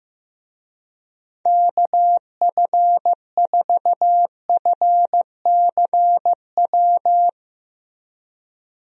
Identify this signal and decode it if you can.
Morse code, heard as a keyed tone, about 15 wpm, 700 Hz. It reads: KF4FCW
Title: USA Call Signs - 6 Characters - Repeated 3x - 15wpm